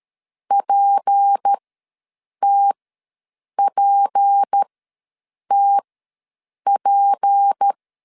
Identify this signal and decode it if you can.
Morse code, heard as a keyed tone, about 13 wpm, spaced as for 7 wpm, 800 Hz. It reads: PTPTP
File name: LEZIONI / 6 PKX0 / 6 P ws 12wpm 3117